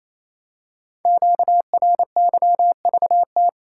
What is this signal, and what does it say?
Morse code, heard as a keyed tone, about 29 wpm, 700 Hz. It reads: QRYVT
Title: Code Groups - 5 Letter:Numbers - 28wpm